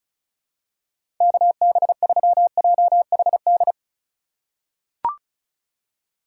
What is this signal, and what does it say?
Morse code, heard as a keyed tone, about 34 wpm, 700 Hz. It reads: KB3JHD
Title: USA Call Signs - 6 Characters - 3x Character Spacing - 35wpm